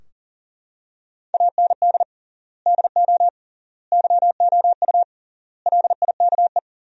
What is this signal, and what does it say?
Morse code, heard as a keyed tone, about 40 wpm, 700 Hz. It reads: AND DO YOU LIKE
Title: 3x Word-Spacing - Sentences from Top 100 Words - 40wpm